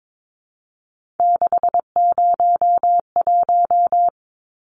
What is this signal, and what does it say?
Morse code, heard as a keyed tone, about 22 wpm, 700 Hz. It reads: 601
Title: Numbers - 3-Digit 22wpm